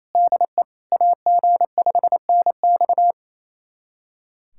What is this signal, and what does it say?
Morse code, heard as a keyed tone, about 28 wpm, 700 Hz. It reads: DE AG5NX
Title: USA Callsign DE Callsign - 5 Characters - 28wpm